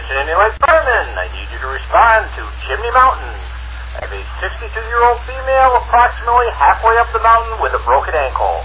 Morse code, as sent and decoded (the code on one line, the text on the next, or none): none